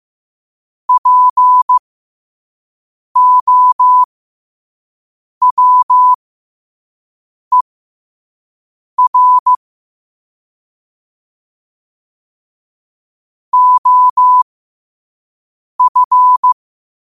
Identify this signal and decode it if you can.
Morse code, heard as a keyed tone, about 15 wpm, spaced as for 5 wpm, 1000 Hz. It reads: POWER OF
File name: qotdfiles / 5wpm